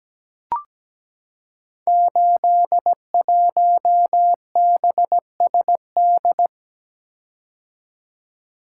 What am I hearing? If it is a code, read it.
Morse code, heard as a keyed tone, about 17 wpm, 700 Hz. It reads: 81BSD
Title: Code Groups - 5 Letter:Numbers - 17wpm